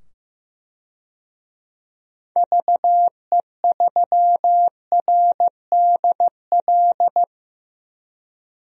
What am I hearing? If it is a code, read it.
Morse code, heard as a keyed tone, about 15 wpm, 700 Hz. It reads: VE3RDL
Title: DXCC-100 Call Signs - 6 Characters - 15wpm